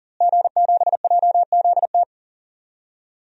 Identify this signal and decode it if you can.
Morse code, heard as a keyed tone, about 40 wpm, 700 Hz. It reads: G7JZT
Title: DXCC-100 Call Signs - 5 Characters - Repeated 2x - 40wpm